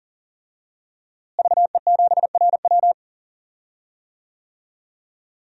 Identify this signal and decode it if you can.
Morse code, heard as a keyed tone, about 40 wpm, 700 Hz. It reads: VE7RW
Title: DXCC-100 Call Signs - 5 Characters - Repeated 3x - 40wpm